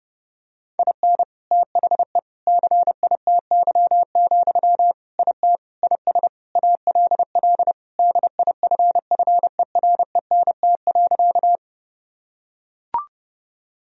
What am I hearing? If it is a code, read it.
Morse code, heard as a keyed tone, about 32 wpm, 700 Hz. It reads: IN THE CITY, IT IS ALL DIFFERENT.
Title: Sentences from Top 1000 Words - Review Part 3 - 30wpm